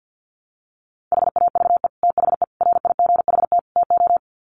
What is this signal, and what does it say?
Morse code, heard as a keyed tone, about 50 wpm, 700 Hz. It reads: HAVE THE RIGHT TO
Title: Sets of 4 Words 50wpm